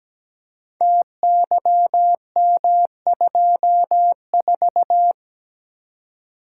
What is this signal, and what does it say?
Morse code, heard as a keyed tone, about 17 wpm, 700 Hz. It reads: TYM24
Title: Code Groups - 5 Letter:Numbers - 17wpm